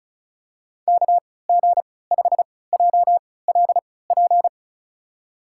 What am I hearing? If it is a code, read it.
Morse code, heard as a keyed tone, about 36 wpm, spaced as for 20 wpm, 700 Hz. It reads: KG5JLP
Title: USA Call Signs - 3x Character Spacing - 35wpm